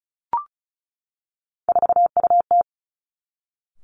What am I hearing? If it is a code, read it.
Morse code, heard as a keyed tone, about 36 wpm, 700 Hz. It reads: 4UT
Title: Three Challenging Letter-Number Combinations 35wpm